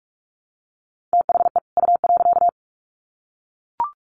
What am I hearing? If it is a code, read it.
Morse code, heard as a keyed tone, about 44 wpm, 700 Hz. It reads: THE U.
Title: Sets of 2 Words 45wpm